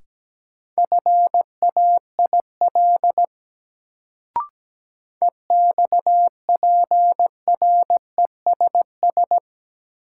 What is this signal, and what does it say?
Morse code, heard as a keyed tone, about 17 wpm, 700 Hz. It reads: FAIL EXPRESS
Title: Sentences from Top 1100 Words - New Word Drill Alt - 17wpm